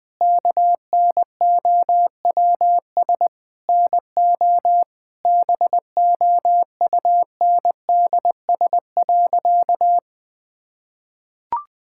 Morse code, heard as a keyed tone, about 20 wpm, 700 Hz. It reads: KNOWS NO BOUNDS.
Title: Difficult Sentences with Hints - 20wpm